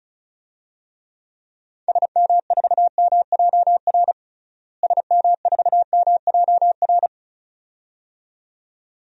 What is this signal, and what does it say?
Morse code, heard as a keyed tone, about 36 wpm, 700 Hz. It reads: SM4MJR SM4MJR
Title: DXCC-100 Call Signs - 6 Characters - Repeated 2x - 35wpm